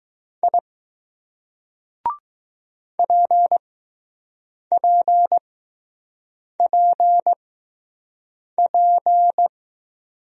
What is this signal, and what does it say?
Morse code, heard as a keyed tone, about 20 wpm, spaced as for 6 wpm, 700 Hz. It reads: I PPPP